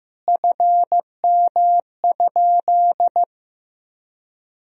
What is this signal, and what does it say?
Morse code, heard as a keyed tone, about 16 wpm, 700 Hz. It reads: FM?
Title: Sentences from Top 300 Words and Abbreviations 15wpm